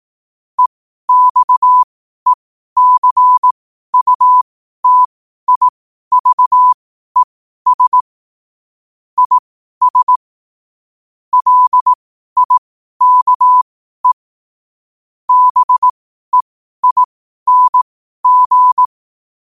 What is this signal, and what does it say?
Morse code, heard as a keyed tone, about 18 wpm, spaced as for 12 wpm, 1000 Hz. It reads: EXECUTIVES IS LIKE BEING
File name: qotdfiles / 12wpm